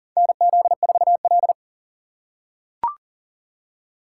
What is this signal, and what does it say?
Morse code, heard as a keyed tone, about 40 wpm, 700 Hz. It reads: NZ4L
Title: USA Call Signs - 2x Character Spacing - 40wpm